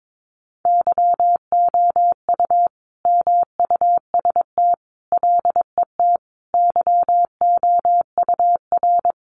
Morse code, heard as a keyed tone, about 22 wpm, 700 Hz. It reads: YOU MUST LET YOUR
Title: Sentences from Top 1500 Words 22wpm